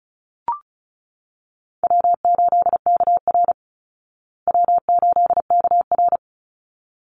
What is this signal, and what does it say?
Morse code, heard as a keyed tone, about 34 wpm, 700 Hz. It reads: W8KR W8KR
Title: Call Signs Repeated 3x - 35wpm